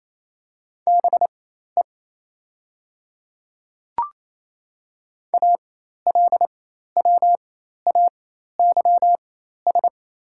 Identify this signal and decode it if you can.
Morse code, heard as a keyed tone, about 28 wpm, spaced as for 13 wpm, 700 Hz. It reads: BE ALWAYS
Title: 4x Character Spacing - Top 300 Words - 28wpm